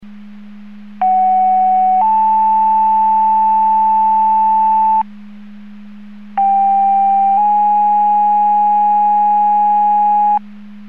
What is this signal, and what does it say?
Two-tone page, heard as a one-second tone, then a three-second tone, about 750 then 880 Hz, about 790 then 830 Hz. It is heard from 1.0 to 10.4 s.